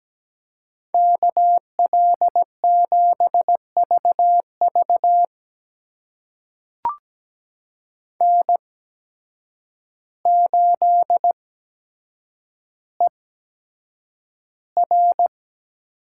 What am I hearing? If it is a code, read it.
Morse code, heard as a keyed tone, about 17 wpm, spaced as for 8 wpm, 700 Hz. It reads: KL7VV N8ER
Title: USA Call Signs - 5 Characters - 8x Character Spacing - 17wpm